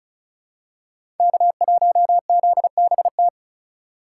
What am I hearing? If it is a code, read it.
Morse code, heard as a keyed tone, about 34 wpm, 700 Hz. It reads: K1ZBT
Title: USA Call Signs - 5 Characters - 2x Character Spacing - 35wpm